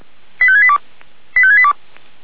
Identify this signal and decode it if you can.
Five-tone selective call (ZVEI), heard as quick runs of five tones, tones of about 70 ms. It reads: 85682 85682